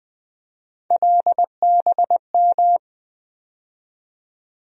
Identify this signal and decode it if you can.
Morse code, heard as a keyed tone, about 20 wpm, 700 Hz. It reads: LBM
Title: USA Call Sign Postfixes - 3 Letters 20wpm